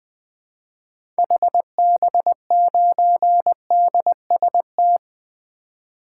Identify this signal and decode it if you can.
Morse code, heard as a keyed tone, about 20 wpm, 700 Hz. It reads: HB9DST
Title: 2023 SOTA Activator Call Signs - 20wpm